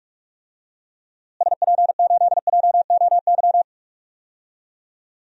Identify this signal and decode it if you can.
Morse code, heard as a keyed tone, about 43 wpm, 700 Hz. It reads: SP8JOY
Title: DXCC-100 Call Signs - 6 Characters - Repeated 2x - 45wpm